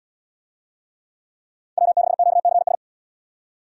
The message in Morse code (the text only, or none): W6QCS